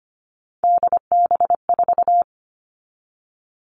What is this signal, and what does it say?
Morse code, heard as a keyed tone, about 25 wpm, 700 Hz. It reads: DB4
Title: All Cut Numbers - 3-Digit 25wpm